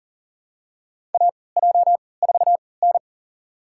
Morse code, heard as a keyed tone, about 40 wpm, spaced as for 23 wpm, 700 Hz. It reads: AJ4N
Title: USA Call Signs - 3x Character Spacing - 40wpm